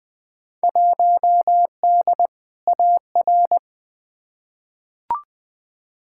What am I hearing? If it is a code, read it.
Morse code, heard as a keyed tone, about 20 wpm, 700 Hz. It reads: 1D AR
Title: Field Day Exchanges - 20wpm